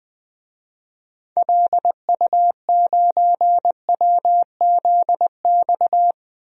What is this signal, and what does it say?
Morse code, heard as a keyed tone, about 21 wpm, 700 Hz. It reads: LU9WZX
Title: DXCC-100 Call Signs - 6 Characters - Repeated 3x - 20wpm